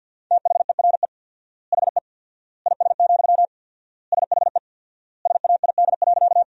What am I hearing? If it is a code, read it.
Morse code, heard as a keyed tone, about 46 wpm, 700 Hz. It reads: THERE HE IS, SHE SAID.
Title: 4x Word-Spacing - Sentences from Top 100 Words - 50wpm